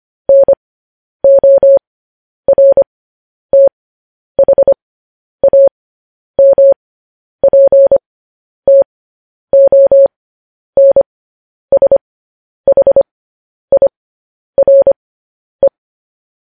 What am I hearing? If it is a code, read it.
Morse code, heard as a keyed tone, about 25 wpm, spaced as for 10 wpm, 550 Hz. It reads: NORTHAMPTONSHIRE
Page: news headlines in morse code at 10 wpm